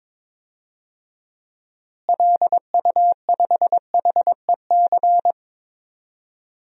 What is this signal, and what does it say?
Morse code, heard as a keyed tone, about 22 wpm, 700 Hz. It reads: LU5HEC
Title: DXCC-100 Call Signs - 6 Characters - Repeated 2x - 22wpm